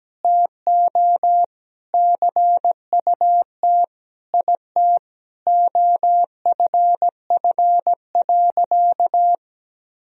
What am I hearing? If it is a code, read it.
Morse code, heard as a keyed tone, about 17 wpm, 700 Hz. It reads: TO CUT IT OFF.
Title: Sentences from Top 1000 Words - Review Part 1 - 17wpm